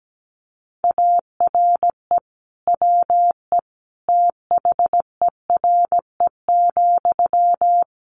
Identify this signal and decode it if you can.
Morse code, heard as a keyed tone, about 17 wpm, 700 Hz. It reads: ARE WE THERE,
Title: Sentences from Top 1300 Words 17wpm